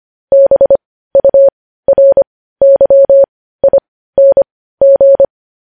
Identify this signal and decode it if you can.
Morse code, heard as a keyed tone, about 25 wpm, spaced as for 15 wpm, 550 Hz. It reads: BURYING
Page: news headlines in morse code at 15 wpm